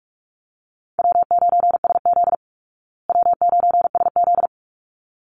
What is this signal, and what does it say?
Morse code, heard as a keyed tone, about 44 wpm, 700 Hz. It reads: W9SZ W9SZ
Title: Call Signs Repeated 3x - 45wpm